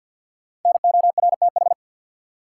Morse code, heard as a keyed tone, about 47 wpm, 700 Hz. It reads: NORTH
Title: Sentences from Top 400 Words - New Word Drill - 50wpm